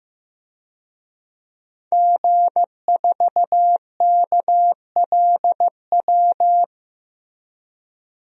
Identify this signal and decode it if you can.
Morse code, heard as a keyed tone, about 15 wpm, 700 Hz. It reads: G4KLW
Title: DXCC-100 Call Signs - 5 Characters - Repeated 3x - 15wpm